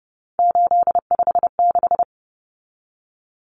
Morse code, heard as a keyed tone, about 29 wpm, 700 Hz. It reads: 856
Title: Numbers - 3-Digit 30wpm